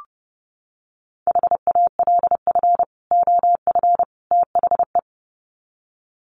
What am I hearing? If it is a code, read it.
Morse code, heard as a keyed tone, about 30 wpm, 700 Hz. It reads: HALF OF THE
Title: Sets of 3 Words 30wpm